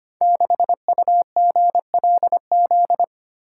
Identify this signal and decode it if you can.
Morse code, heard as a keyed tone, about 25 wpm, 700 Hz. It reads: 6UGLZ
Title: Code Groups - 5 Letter:Numbers - 25wpm